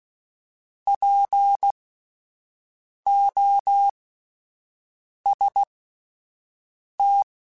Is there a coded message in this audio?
Yes: Morse code from 0.9 to 7.2 s, heard as a keyed tone, about 16 wpm, spaced as for 5 wpm, 800 Hz. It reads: POST